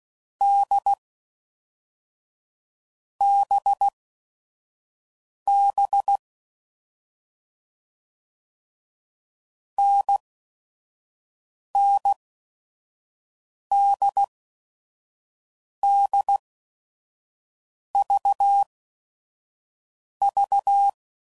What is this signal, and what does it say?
Morse code, heard as a keyed tone, about 16 wpm, spaced as for 5 wpm, 800 Hz. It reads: D BB NNDDVV
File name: Mixed / Short Runs / NDB NDB6